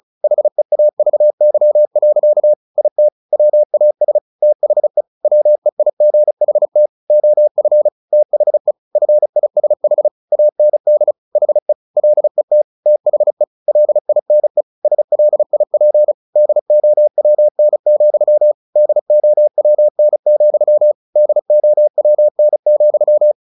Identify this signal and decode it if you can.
Morse code, heard as a keyed tone, about 34 wpm, 600 Hz. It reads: HEAVY. IT WAS THE WEIGHT OF THE FISH AND HE LET THE LINE SLIP DOWN, DOWN, DOWN,
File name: the old man and the sea / the old man and the sea 1 35wpm